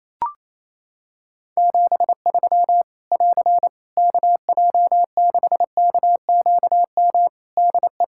Morse code, heard as a keyed tone, about 28 wpm, 700 Hz. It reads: 73 <AR> KJ6KQM DE
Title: K1USN SST QSO Practice 28wpm